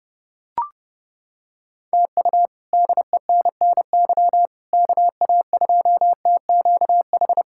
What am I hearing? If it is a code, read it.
Morse code, heard as a keyed tone, about 32 wpm, 700 Hz. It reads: TU DENNY KA2TQH
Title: K1USN SST QSO Practice - 30wpm